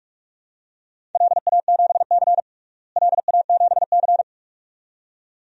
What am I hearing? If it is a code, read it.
Morse code, heard as a keyed tone, about 46 wpm, 700 Hz. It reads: LA7C LA7C
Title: DXCC-100 Call Signs - 4 Characters - Repeated 2x - 45wpm